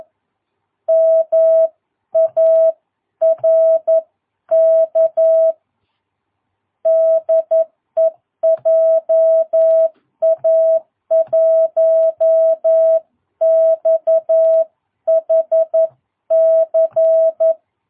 Morse code, heard as a keyed tone, about 11 wpm, 660 Hz. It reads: MARK DE JA1XHC
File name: jh8cbh / 2022cw